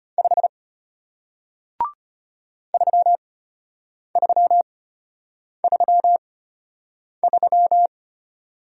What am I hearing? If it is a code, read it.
Morse code, heard as a keyed tone, about 33 wpm, spaced as for 8 wpm, 700 Hz. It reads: 5 3333